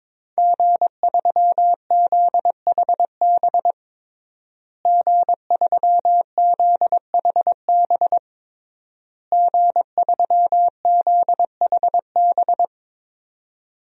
Morse code, heard as a keyed tone, about 22 wpm, 700 Hz. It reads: G3ZHB G3ZHB G3ZHB
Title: DXCC-100 Call Signs - 5 Characters - Repeated 3x - 22wpm